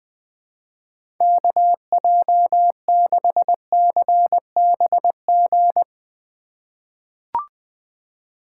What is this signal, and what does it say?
Morse code, heard as a keyed tone, about 20 wpm, 700 Hz. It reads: KJ6CBG
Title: USA Call Signs - 6 Characters - 2x Character Spacing - 20wpm